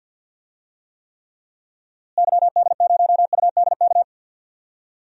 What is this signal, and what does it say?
Morse code, heard as a keyed tone, about 52 wpm, 700 Hz. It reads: YD0UDK
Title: DXCC-100 Call Signs - 6 Characters - Repeated 3x - 50wpm